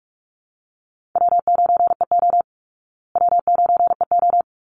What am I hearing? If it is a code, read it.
Morse code, heard as a keyed tone, about 44 wpm, 700 Hz. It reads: W9EO W9EO
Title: Call Signs Repeated 3x - 45wpm